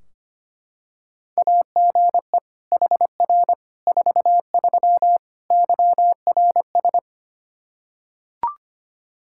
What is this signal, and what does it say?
Morse code, heard as a keyed tone, about 25 wpm, 700 Hz. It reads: AGE HR 43 YRS
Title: Practice QSO - Exchange 3 - More Info - 25wpm